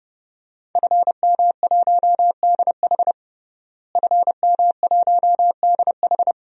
Morse code, heard as a keyed tone, about 30 wpm, 700 Hz. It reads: FM1DH FM1DH
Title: DXCC-100 Call Signs - 5 Characters - Repeated 2x - 30wpm